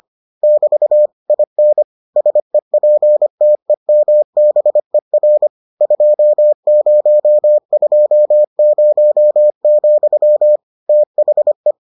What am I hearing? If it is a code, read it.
Morse code, heard as a keyed tone, about 25 wpm, 600 Hz. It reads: <BT> IN SEPTEMBER 2020, THE